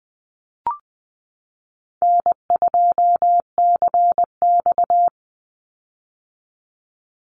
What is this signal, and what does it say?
Morse code, heard as a keyed tone, about 20 wpm, 700 Hz. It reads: N2CX
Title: Call Signs - 20wpm